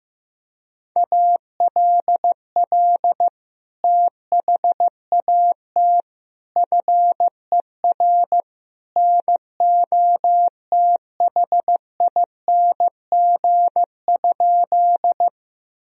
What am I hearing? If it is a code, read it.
Morse code, heard as a keyed tone, about 15 wpm, 700 Hz. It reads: ALL THAT FER NOTHING?
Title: Sentences from Top 500 Words and Abbreviations 15wpm